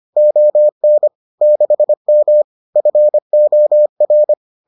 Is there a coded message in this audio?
Yes: Morse code from 0.2 to 4.3 s, heard as a keyed tone, about 25 wpm, 600 Hz. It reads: ON 6M FOR